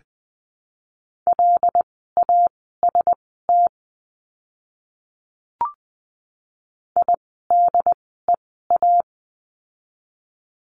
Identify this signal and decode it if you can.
Morse code, heard as a keyed tone, about 20 wpm, spaced as for 14 wpm, 700 Hz. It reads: LAST IDEA
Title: 108 Lesson 18f - Letter m - Words 20wpm